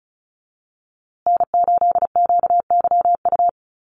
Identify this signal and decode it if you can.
Morse code, heard as a keyed tone, about 34 wpm, 700 Hz. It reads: N8QYU
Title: USA Call Signs Repeated 2x 35wpm